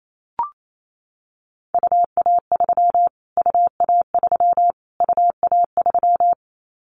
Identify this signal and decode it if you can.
Morse code, heard as a keyed tone, about 28 wpm, 700 Hz. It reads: UA3 UA3 UA3
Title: Callsign Prefixes for Top 25 DXCC - Repeated 3x - 28wpm